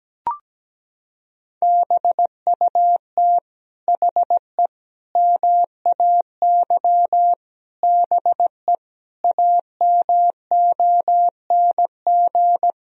Morse code, heard as a keyed tone, about 17 wpm, 700 Hz. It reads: BUT HE MAY BE AMONG
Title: Sentences from Top 1000 Words - Review Part 3 - 17wpm